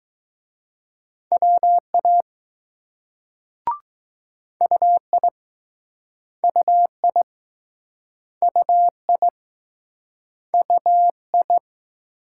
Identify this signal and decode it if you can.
Morse code, heard as a keyed tone, about 20 wpm, 700 Hz. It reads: WA UI UI UI UI